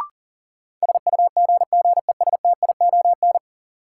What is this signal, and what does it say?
Morse code, heard as a keyed tone, about 40 wpm, 700 Hz. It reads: SUGGESTION
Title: Sentences from Top 1600 Words - New Word Drill Alt - 40wpm